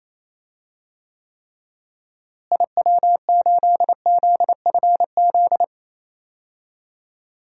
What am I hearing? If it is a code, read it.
Morse code, heard as a keyed tone, about 28 wpm, 700 Hz. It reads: IW8ZFZ
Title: DXCC-100 Call Signs - 6 Characters - Repeated 3x - 28wpm